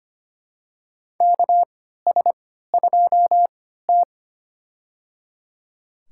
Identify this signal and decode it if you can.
Morse code, heard as a keyed tone, about 25 wpm, spaced as for 14 wpm, 700 Hz. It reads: KS2T
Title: USA Call Signs - 4 Characters - 3x Character Spacing - 25wpm